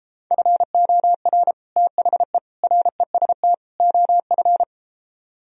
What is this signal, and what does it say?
Morse code, heard as a keyed tone, about 33 wpm, 700 Hz. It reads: FOR THE REST OF